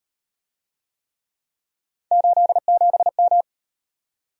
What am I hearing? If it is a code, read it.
Morse code, heard as a keyed tone, about 36 wpm, 700 Hz. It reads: 87M